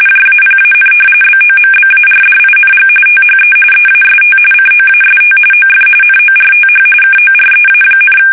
RTTY, about 45.45 baud. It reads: JAPAN BOMBARDING THE EAST COAST OF NORTH KOREA,
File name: fsk850 / battleshipnj